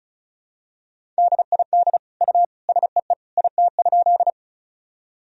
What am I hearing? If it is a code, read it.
Morse code, heard as a keyed tone, about 36 wpm, 700 Hz. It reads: DID U SEE IT?
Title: Sentences from Top 100 Words and Abbreviations 35wpm